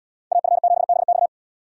Morse code, heard as a keyed tone, about 68 wpm, 700 Hz. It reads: AF7BX